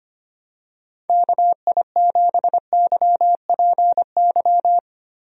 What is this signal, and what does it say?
Morse code, heard as a keyed tone, about 25 wpm, 700 Hz. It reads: KI7YPY
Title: USA Call Signs - 6 Characters - Repeated 2x - 25wpm